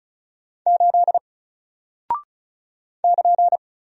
Morse code, heard as a keyed tone, about 35 wpm, 700 Hz. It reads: 8 <KN>
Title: Single Character (Letter:Number:Punctuation:Prosign) Alt 35wpm